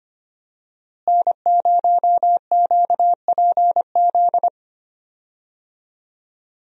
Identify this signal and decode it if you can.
Morse code, heard as a keyed tone, about 25 wpm, 700 Hz. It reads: N0QPZ
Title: USA Call Signs - 5 Characters - Repeated 3x - 25wpm